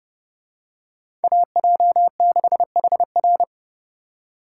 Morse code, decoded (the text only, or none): AJ6HR